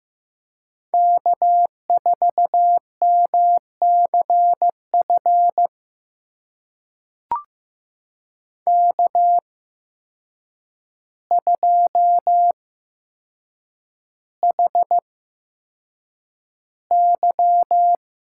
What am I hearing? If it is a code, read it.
Morse code, heard as a keyed tone, about 15 wpm, spaced as for 7 wpm, 700 Hz. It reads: K4MCF K2HY
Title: USA Call Signs - 5 Characters - 8x Character Spacing - 15wpm